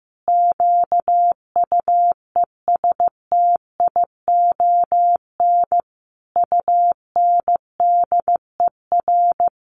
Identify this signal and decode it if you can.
Morse code, heard as a keyed tone, about 15 wpm, 700 Hz. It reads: QUESTION UNDER
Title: Sentences from Top 1500 Words 15wpm